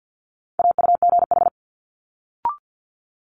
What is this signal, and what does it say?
Morse code, heard as a keyed tone, about 50 wpm, 700 Hz. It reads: AUGH
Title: Multi-letter Phonemes Quick Fire 50wpm